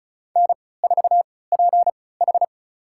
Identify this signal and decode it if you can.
Morse code, heard as a keyed tone, about 34 wpm, spaced as for 20 wpm, 700 Hz. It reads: N4PH
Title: USA Call Signs - 4 Characters - 3x Character Spacing - 35wpm